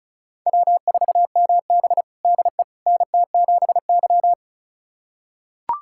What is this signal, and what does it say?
Morse code, heard as a keyed tone, about 36 wpm, 700 Hz. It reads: W4MB DE NT7Y
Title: USA Callsign DE Callsign - 4 Characters - 35wpm